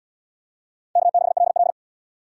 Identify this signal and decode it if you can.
Morse code, heard as a keyed tone, about 70 wpm, 700 Hz. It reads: K6LL